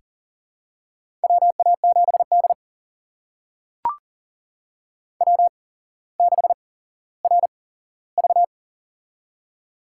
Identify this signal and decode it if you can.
Morse code, heard as a keyed tone, about 40 wpm, spaced as for 17 wpm, 700 Hz. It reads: WA7D W6RV
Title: USA Call Signs - 4 Characters - 8x Character Spacing - 40wpm